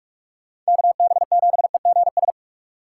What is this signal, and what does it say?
Morse code, heard as a keyed tone, about 46 wpm, 700 Hz. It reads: KB7EGS
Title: USA Call Signs - 6 Characters - 2x Character Spacing - 45wpm